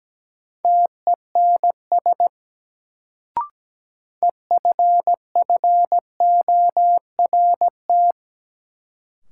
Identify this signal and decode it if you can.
Morse code, heard as a keyed tone, about 17 wpm, 700 Hz. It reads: TENS EFFORT
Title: Sets of 4 Words - New Word Drill - 17wpm